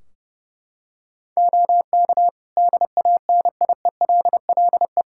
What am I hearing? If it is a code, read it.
Morse code, heard as a keyed tone, about 31 wpm, 700 Hz. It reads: OK DANIELLE
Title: Practice QSO - Exchange 3 - More Info - 30wpm